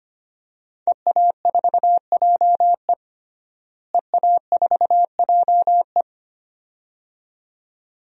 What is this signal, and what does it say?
Morse code, heard as a keyed tone, about 25 wpm, 700 Hz. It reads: EA4JE EA4JE
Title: DXCC-100 Call Signs - 5 Characters - Repeated 2x - 25wpm